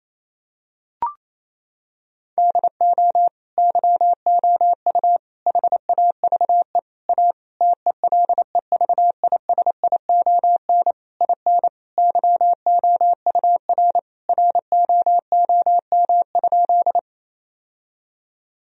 Morse code, heard as a keyed tone, about 27 wpm, 700 Hz. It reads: DO YOU HAVE A TELEVISION IN YOUR ROOM?
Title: Sentences from Top 1000 Words - Review Part 4 - 28wpm